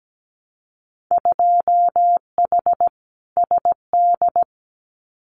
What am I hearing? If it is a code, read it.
Morse code, heard as a keyed tone, about 17 wpm, 700 Hz. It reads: 2H SD
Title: Winter Field Day Exchanges 17wpm